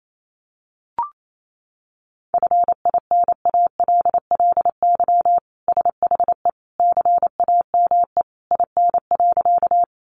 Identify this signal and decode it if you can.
Morse code, heard as a keyed tone, about 28 wpm, 700 Hz. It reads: FINALLY SHE CAME IN.
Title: Sentences from Top 1000 Words 28wpm